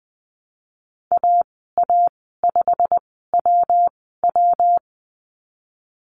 Morse code, heard as a keyed tone, about 20 wpm, spaced as for 15 wpm, 700 Hz. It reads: AA5WW
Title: 128 Lesson 21g - Number 6 - Call Signs 20wpm